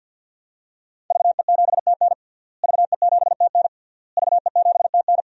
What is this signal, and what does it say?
Morse code, heard as a keyed tone, about 50 wpm, 700 Hz. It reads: VE7TN VE7TN VE7TN
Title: DXCC-100 Call Signs - 5 Characters - Repeated 3x - 50wpm